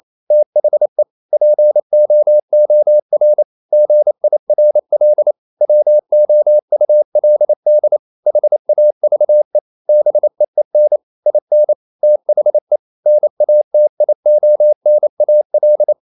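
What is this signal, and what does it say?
Morse code, heard as a keyed tone, about 28 wpm, 600 Hz. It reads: THE POOR GIRL WOULD HAVE BEEN IN THE NATIONAL
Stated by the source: Morse code practice at 28 wpm, text of Ken Follett's novel Never in English, from CW med Gnister.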